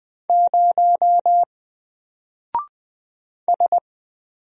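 Morse code, heard as a keyed tone, about 20 wpm, 700 Hz. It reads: 0 S
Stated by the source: Single letter and number Morse drill at 20 wpm, from N4LCD.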